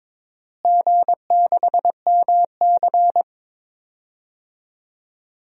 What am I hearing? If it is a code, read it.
Morse code, heard as a keyed tone, about 22 wpm, 700 Hz. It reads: G6MC
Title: DXCC-100 Call Signs - 4 Characters - Repeated 3x - 22wpm